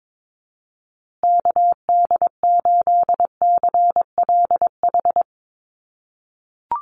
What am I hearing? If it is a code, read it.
Morse code, heard as a keyed tone, about 22 wpm, 700 Hz. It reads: KD8CLH
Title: USA Call Signs Repeated 2x 22wpm